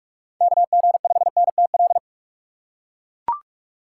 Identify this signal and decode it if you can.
Morse code, heard as a keyed tone, about 43 wpm, 700 Hz. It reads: KG5NTL
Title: USA Call Signs - 6 Characters - 6x Character Spacing - 45wpm